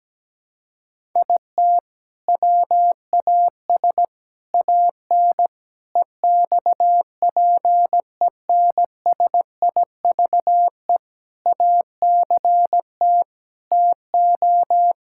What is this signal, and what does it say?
Morse code, heard as a keyed tone, about 17 wpm, 700 Hz. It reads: IT WAS AN EXPENSIVE ACT TO
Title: Sentences from Top 1000 Words - Review Part 6 - 17wpm